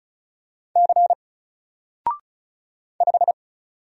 Morse code, heard as a keyed tone, about 37 wpm, 700 Hz. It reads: C 5
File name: Single-Letter-Number-35wpm